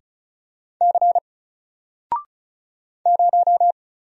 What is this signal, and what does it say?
Morse code, heard as a keyed tone, about 36 wpm, 700 Hz. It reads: C 0
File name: Single-Letter-Number-35wpm